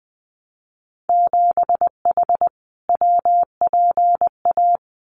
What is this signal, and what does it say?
Morse code, heard as a keyed tone, about 20 wpm, 700 Hz. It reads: 7H WPA
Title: Winter Field Day Exchanges Encore 20wpm